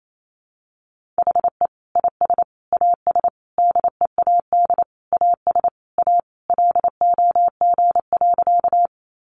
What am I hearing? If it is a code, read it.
Morse code, heard as a keyed tone, about 28 wpm, 700 Hz. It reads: HE IS AS DEAD AS A LOG.
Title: Sentences from Top 1500 Words 28wpm